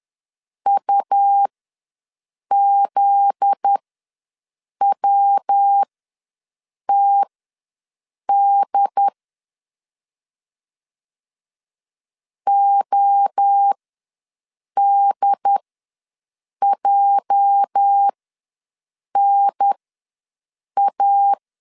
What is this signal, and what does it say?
Morse code, heard as a keyed tone, about 10 wpm, spaced as for 6 wpm, 800 Hz. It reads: UZWTD ODJNA
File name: LEZIONI / 8 YJ45 / 8 YJ ws 10wpm 4043